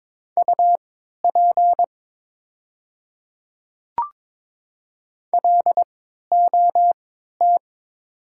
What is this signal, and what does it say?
Morse code, heard as a keyed tone, about 22 wpm, spaced as for 12 wpm, 700 Hz. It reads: UP LOT
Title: 3x Character Spacing - Top 500 Words - 22wpm